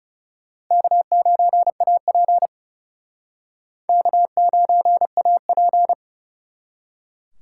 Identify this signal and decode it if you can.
Morse code, heard as a keyed tone, about 33 wpm, 700 Hz. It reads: K9AP K9AP
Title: Speed Racing - USA Call Signs 30wpm